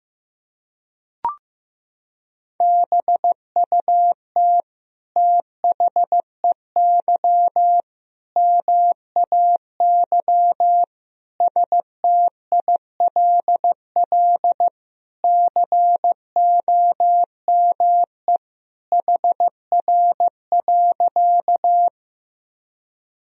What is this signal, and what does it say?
Morse code, heard as a keyed tone, about 16 wpm, 700 Hz. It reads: BUT THEY MAY STILL COME HR.
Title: Sentences from Top 300 Words and Abbreviations 15wpm